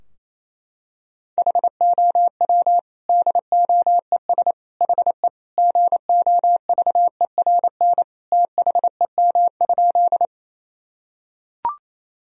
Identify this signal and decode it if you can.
Morse code, heard as a keyed tone, about 28 wpm, 700 Hz. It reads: HOW DOES HE GOVERN THEM?
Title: Sentences from Top 1000 Words - Review Part 2 - 28wpm